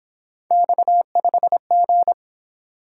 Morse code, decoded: X5G

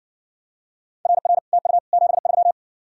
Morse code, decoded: RR TU 73

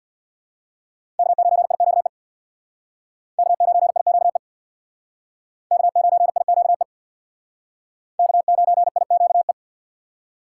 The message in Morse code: K9IQE K9IQE K9IQE K9IQE

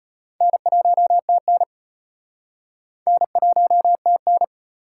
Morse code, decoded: N1TN N1TN

